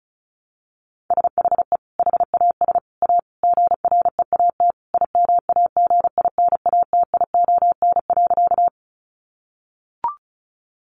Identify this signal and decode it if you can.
Morse code, heard as a keyed tone, about 34 wpm, 700 Hz. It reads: SHE HAS A GREAT IMAGINATION.
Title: Sentences from Top 1300 Words 35wpm